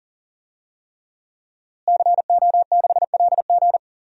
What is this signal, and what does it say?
Morse code, heard as a keyed tone, about 40 wpm, 700 Hz. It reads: CO6LG
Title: DXCC-100 Call Signs - 5 Characters - Repeated 3x - 40wpm